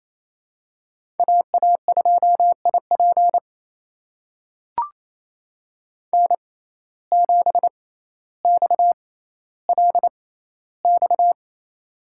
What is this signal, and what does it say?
Morse code, heard as a keyed tone, about 28 wpm, spaced as for 14 wpm, 700 Hz. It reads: AA2IP N7XLX
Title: USA Call Signs - 5 Characters - 6x Character Spacing - 28wpm